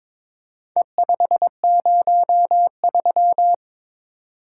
Morse code, heard as a keyed tone, about 22 wpm, 700 Hz. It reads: E503
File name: practice / 22wpm / Four Letter-Number Combinations - 22wpm